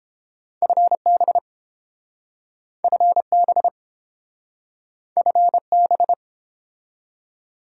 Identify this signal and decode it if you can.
Morse code, heard as a keyed tone, about 30 wpm, 700 Hz. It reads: FB FB FB